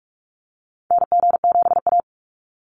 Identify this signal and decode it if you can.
Morse code, heard as a keyed tone, about 46 wpm, 700 Hz. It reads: NG7A
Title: NA SOTA Activator Callsigns 45wpm